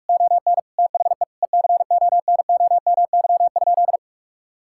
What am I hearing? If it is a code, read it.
Morse code, heard as a keyed tone, about 46 wpm, 700 Hz. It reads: ON THE ECONOMY?
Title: Sentences from Top 1000 Words - Review Part 6 - 45wpm